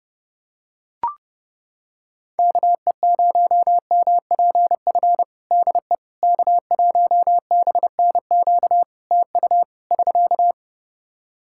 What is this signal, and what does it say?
Morse code, heard as a keyed tone, about 30 wpm, 700 Hz. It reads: KE0MPF DE K1BNQ TU <SK>
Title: Practice QSO - Exchange 4 - Ending - 30wpm